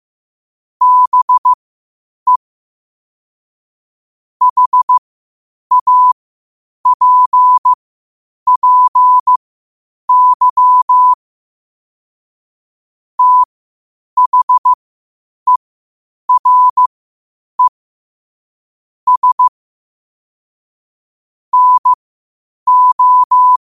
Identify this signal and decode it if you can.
Morse code, heard as a keyed tone, about 15 wpm, spaced as for 8 wpm, 1000 Hz. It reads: BE HAPPY THERE S NO
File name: qotdfiles / 8wpm